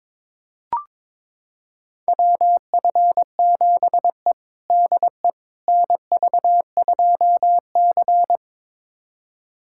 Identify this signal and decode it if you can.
Morse code, heard as a keyed tone, about 22 wpm, 700 Hz. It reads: WF7E DE NV2C
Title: USA Callsign DE Callsign - 4 Characters - 22wpm